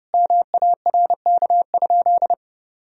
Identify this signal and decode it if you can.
Morse code, heard as a keyed tone, about 30 wpm, 700 Hz. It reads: MARK?
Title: Sentences from Top 1000 Words - Review Part 1 - 30wpm